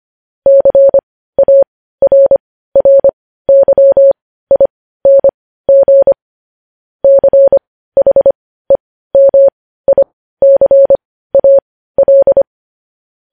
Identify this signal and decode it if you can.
Morse code, heard as a keyed tone, about 25 wpm, spaced as for 15 wpm, 550 Hz. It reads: CARRYING CHEMICAL
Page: news headlines in morse code at 15 wpm